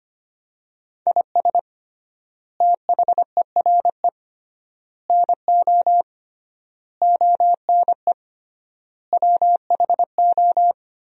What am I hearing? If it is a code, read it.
Morse code, heard as a keyed tone, about 25 wpm, 700 Hz. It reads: IS THERE NO ONE WHO
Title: 3x Word-Spacing - Sentences from Top 200 Words - 25wpm